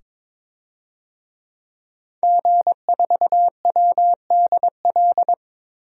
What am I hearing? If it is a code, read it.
Morse code, heard as a keyed tone, about 23 wpm, 700 Hz. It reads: G4WDL
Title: DXCC-100 Call Signs - 5 Characters - 22wpm